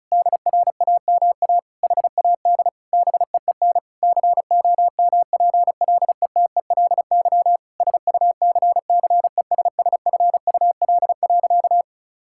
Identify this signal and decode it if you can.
Morse code, heard as a keyed tone, about 36 wpm, 700 Hz. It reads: DRAMA HAD BEEN COMPLETELY SUCCESSFUL.
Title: Sentences from Top 1600 Words 35wpm